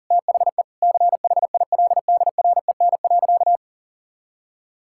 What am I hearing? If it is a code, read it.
Morse code, heard as a keyed tone, about 40 wpm, 700 Hz. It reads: THE CHILDREN.
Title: Sentences from Top 1000 Words - Review Part 6 - 40wpm